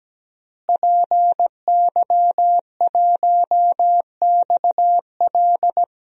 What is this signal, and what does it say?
Morse code, heard as a keyed tone, about 17 wpm, 700 Hz. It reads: PY1XL
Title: DXCC-100 Call Signs - 5 Characters - Repeated 2x - 17wpm